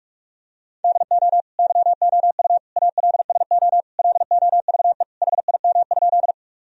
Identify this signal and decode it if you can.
Morse code, heard as a keyed tone, about 41 wpm, 700 Hz. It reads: DO YOU ALSO LOVE HIM?
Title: Sentences from Top 1000 Words - Review Part 1 - 45wpm